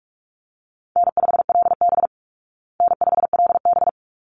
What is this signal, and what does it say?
Morse code, heard as a keyed tone, about 44 wpm, 700 Hz. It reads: N5LB N5LB
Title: Easy Call Signs Repeated 2x - 45wpm